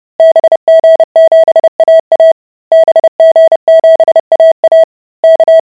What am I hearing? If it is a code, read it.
Morse code, heard as a keyed tone, about 30 wpm, 650 Hz. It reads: BG7AA BG7AA K